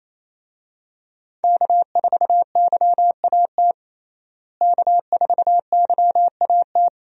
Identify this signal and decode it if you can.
Morse code, heard as a keyed tone, about 27 wpm, 700 Hz. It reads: K4YAT K4YAT
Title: USA Call Signs - 5 Characters - Repeated 2x - 28wpm